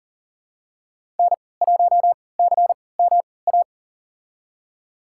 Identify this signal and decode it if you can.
Morse code, heard as a keyed tone, about 40 wpm, spaced as for 23 wpm, 700 Hz. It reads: N1CMA
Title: USA Call Signs - 5 Characters - 3x Character Spacing - 40wpm